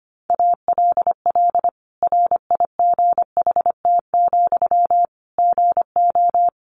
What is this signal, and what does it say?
Morse code, heard as a keyed tone, about 25 wpm, 700 Hz. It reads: ALL RIGHT, GO
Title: Sentences from Top 1100 Words 25wpm